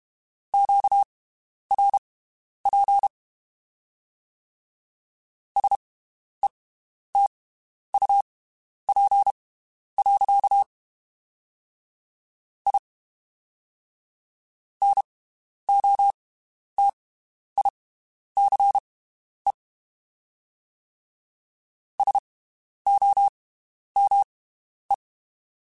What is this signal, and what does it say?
Morse code, heard as a keyed tone, about 32 wpm, spaced as for 11 wpm, 800 Hz. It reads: QRP SETUP. I NOTICE SOME
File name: (spaced) / QSOs Spaced-mp3s / WD0BGP 1 @30 sp